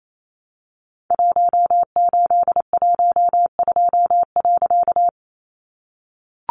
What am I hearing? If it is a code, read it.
Morse code, heard as a keyed tone, about 28 wpm, 700 Hz. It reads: 1812.
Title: A Sound Of Thunder 28wpm